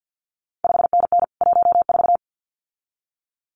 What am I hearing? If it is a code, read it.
Morse code, heard as a keyed tone, about 50 wpm, 700 Hz. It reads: 5NN 14
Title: CQ WW DX Contest Exchange - 50wpm